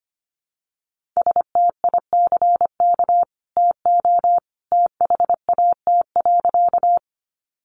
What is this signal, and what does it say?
Morse code, heard as a keyed tone, about 25 wpm, 700 Hz. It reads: STICK TO THAT.
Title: Sentences from Top 1200 Words 25wpm